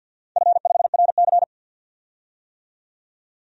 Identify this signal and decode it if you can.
Morse code, heard as a keyed tone, about 48 wpm, 700 Hz. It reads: F5RC
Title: DXCC-100 Call Signs - 4 Characters - 50wpm